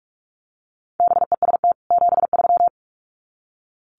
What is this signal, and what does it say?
Morse code, heard as a keyed tone, about 44 wpm, 700 Hz. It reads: BEST 73
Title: Common QSO Phrases 45wpm